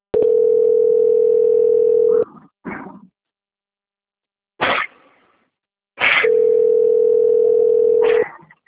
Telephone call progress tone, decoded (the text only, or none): ringback tone